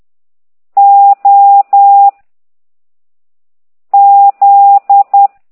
Morse code, heard as a keyed tone, about 10 wpm, 800 Hz. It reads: O Z
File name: ct2idw / exerc13